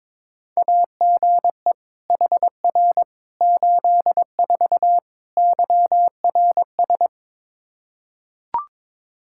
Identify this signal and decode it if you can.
Morse code, heard as a keyed tone, about 22 wpm, 700 Hz. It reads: AGE HR 84 YRS
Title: Practice QSO - Exchange 3 - More Info - 22wpm